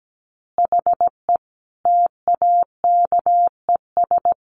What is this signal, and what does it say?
Morse code, heard as a keyed tone, about 17 wpm, 700 Hz. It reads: HE TAKES
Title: Additional Easy Phrases 17wpm